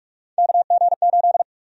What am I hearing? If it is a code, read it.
Morse code, heard as a keyed tone, about 46 wpm, 700 Hz. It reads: KG8
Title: USA Call Sign Prefixes - 45wpm